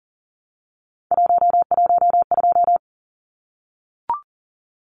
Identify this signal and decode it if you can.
Morse code, heard as a keyed tone, about 40 wpm, 700 Hz. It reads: 112
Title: Numbers - 3-Digit 40wpm